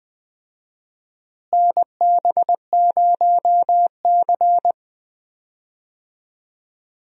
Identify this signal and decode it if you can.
Morse code, heard as a keyed tone, about 20 wpm, 690 Hz. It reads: NB0C
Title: USA Call Signs - 4 Characters - Repeated 3x - 20wpm